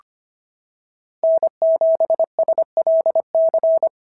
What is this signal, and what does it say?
Morse code, heard as a keyed tone, about 25 wpm, 690 Hz. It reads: N7SLC
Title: SOTA QSO Practice 25wpm